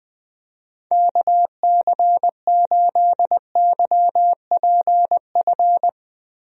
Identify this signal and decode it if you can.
Morse code, heard as a keyed tone, about 20 wpm, 700 Hz. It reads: KC8YPF
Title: USA Call Signs - 6 Characters - Repeated 3x - 20wpm